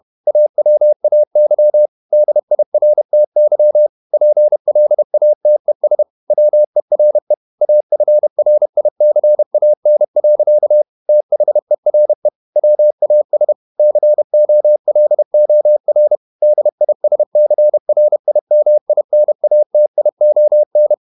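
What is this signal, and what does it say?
Morse code, heard as a keyed tone, about 31 wpm, 600 Hz. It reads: AWAY DIRTY PLATES WERE AFRICAN. THERE WAS COLOR DISCRIMINATION
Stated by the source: Morse code at 30 wpm, the novel Never by Ken Follett in English, from CW med Gnister.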